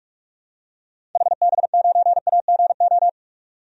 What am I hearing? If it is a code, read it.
Morse code, heard as a keyed tone, about 41 wpm, 700 Hz. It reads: HB9AGO
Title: DXCC-100 Call Signs - 6 Characters - 45wpm